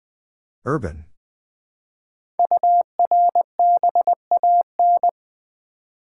Morse code, heard as a keyed tone, about 20 wpm, 700 Hz. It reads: URBAN
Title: 5 Letter Words - First 3 Letters Given - 20wpm